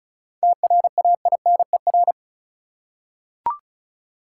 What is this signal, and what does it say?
Morse code, heard as a keyed tone, about 36 wpm, 700 Hz. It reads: TRAINER
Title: Seven Letter Words - 35wpm